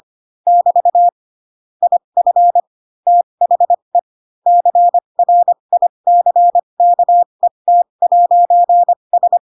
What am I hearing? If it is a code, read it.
Morse code, heard as a keyed tone, about 25 wpm, spaced as for 22 wpm, 700 Hz. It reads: <BT> IF THE CRICKET'S